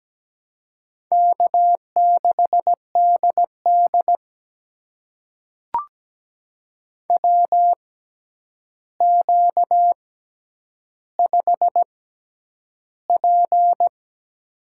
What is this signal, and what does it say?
Morse code, heard as a keyed tone, about 17 wpm, spaced as for 9 wpm, 700 Hz. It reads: K6DD WQ5P
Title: USA Call Signs - 4 Characters - 6x Character Spacing - 17wpm